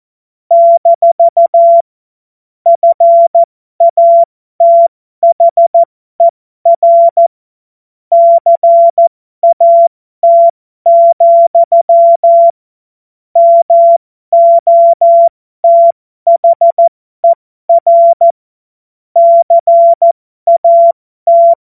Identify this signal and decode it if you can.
Morse code, heard as a keyed tone, about 14 wpm, spaced as for 12 wpm, 680 Hz. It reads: - FATHER CAT, MOTHER CAT
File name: Morsekurs / Witze